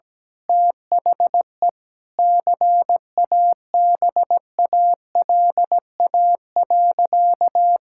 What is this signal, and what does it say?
Morse code, heard as a keyed tone, about 17 wpm, 700 Hz. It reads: THE CABALA.
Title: Pangram Challenge v2 - 17wpm